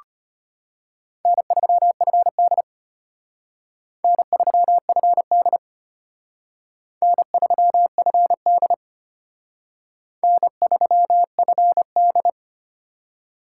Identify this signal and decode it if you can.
Morse code, heard as a keyed tone, about 32 wpm, 700 Hz. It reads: N3FD N3FD N3FD N3FD